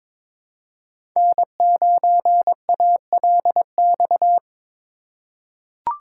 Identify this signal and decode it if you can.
Morse code, heard as a keyed tone, about 22 wpm, 700 Hz. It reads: N9ALX
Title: USA Call Signs - 5 Characters - 4x Character Spacing - 22wpm